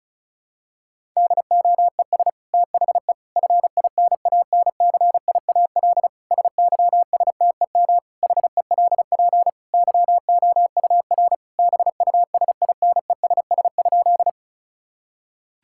Morse code, heard as a keyed tone, about 36 wpm, 700 Hz. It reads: DOES THE FINANCIAL SYSTEM HELP YOUR BUSINESS?
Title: Sentences from Top 1000 Words - Review Part 5 - 35wpm